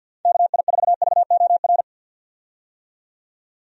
Morse code, heard as a keyed tone, about 47 wpm, 700 Hz. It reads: KI4VOR
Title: USA Call Signs - 6 Characters - 50wpm